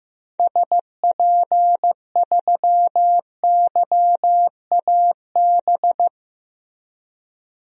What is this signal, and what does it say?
Morse code, heard as a keyed tone, about 16 wpm, 700 Hz. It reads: SP3YAB
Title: DXCC-100 Call Signs - 6 Characters - Repeated 2x - 15wpm